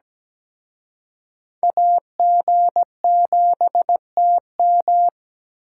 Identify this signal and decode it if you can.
Morse code, heard as a keyed tone, about 20 wpm, 700 Hz. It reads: AG7TM
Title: Speed Racing - USA 2x2 Call Signs 17wpm